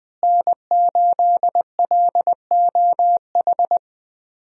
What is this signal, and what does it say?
Morse code, heard as a keyed tone, about 20 wpm, 700 Hz. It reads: N8LOH
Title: USA Call Signs - 5 Characters - Repeated 2x - 20wpm